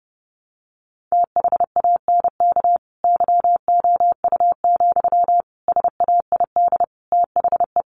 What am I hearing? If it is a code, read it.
Morse code, heard as a keyed tone, about 28 wpm, 700 Hz. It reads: THANK YOU, SAID THE
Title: Sentences from Top 1100 Words 30wpm